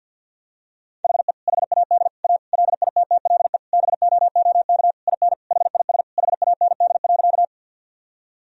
Morse code, heard as a keyed tone, about 51 wpm, 700 Hz. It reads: HE HAD A LITTLE BOOK IN HIS HAND.
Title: Sentences from Top 1000 Words - Review Part 1 - 50wpm